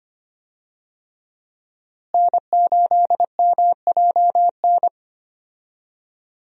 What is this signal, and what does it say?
Morse code, heard as a keyed tone, about 25 wpm, 700 Hz. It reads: N8MJN
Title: USA Call Signs - 5 Characters - Repeated 3x - 25wpm